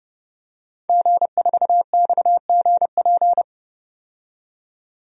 Morse code, heard as a keyed tone, about 30 wpm, 700 Hz. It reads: G4XGP
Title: DXCC-100 Call Signs - 5 Characters - 30wpm